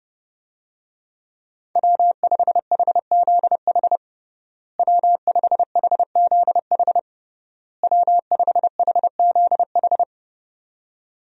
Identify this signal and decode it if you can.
Morse code, heard as a keyed tone, about 32 wpm, 700 Hz. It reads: W5HZH W5HZH W5HZH
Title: USA Call Signs - 5 Characters - Repeated 3x - 30wpm